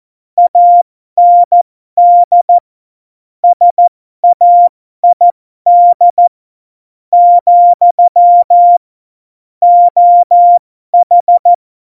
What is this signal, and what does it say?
Morse code, heard as a keyed tone, about 14 wpm, spaced as for 12 wpm, 700 Hz. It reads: AND SAID , OH
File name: Morsekurs / Witze